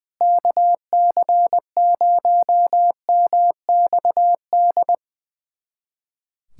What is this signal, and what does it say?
Morse code, heard as a keyed tone, about 20 wpm, 700 Hz. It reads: KC0MXD
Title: USA Callsign DE Callsign - 6 Characters - 20wpm